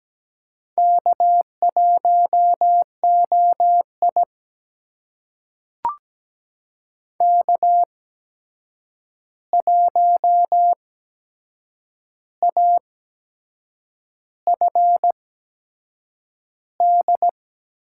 Morse code, heard as a keyed tone, about 17 wpm, spaced as for 7 wpm, 700 Hz. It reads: K1OI K1AFD
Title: USA Call Signs - 8x Character Spacing - 17wpm